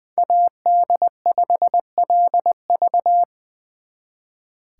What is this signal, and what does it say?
Morse code, heard as a keyed tone, about 23 wpm, 700 Hz. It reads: AD5LV
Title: Speed Racing - USA Call Signs 20wpm